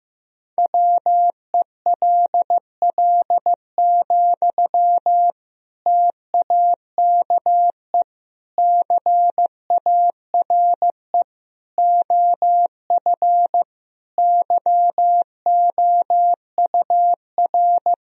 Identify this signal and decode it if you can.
Morse code, heard as a keyed tone, about 16 wpm, 700 Hz. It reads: WELL, TAKE CARE OF YOUR
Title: Sentences from Top 1000 Words - Review Part 1 - 15wpm